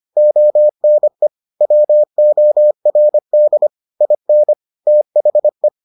Morse code, heard as a keyed tone, about 25 wpm, 600 Hz. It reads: ONE WORD IN THE